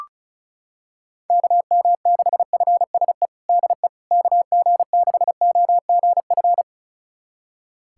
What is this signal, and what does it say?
Morse code, heard as a keyed tone, about 34 wpm, 700 Hz. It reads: KM6FSE DE KG6OGF
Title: USA Callsign DE Callsign - 6 Characters - 35wpm